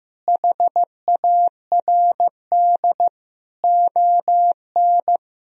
Morse code, heard as a keyed tone, about 15 wpm, 700 Hz. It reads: HARD ON
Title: Sentences from Top 1000 Words - Review Part 1 - 15wpm